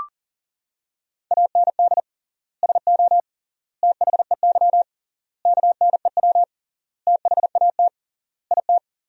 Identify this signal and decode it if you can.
Morse code, heard as a keyed tone, about 40 wpm, 700 Hz. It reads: AND SO THEY KNEW THAT IT
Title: 3x Word-Spacing - Sentences from Top 400 Words - 40wpm